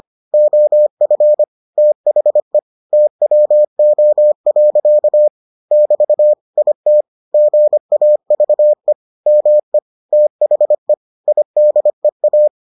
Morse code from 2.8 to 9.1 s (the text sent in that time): TWO. <BT> IT GAVE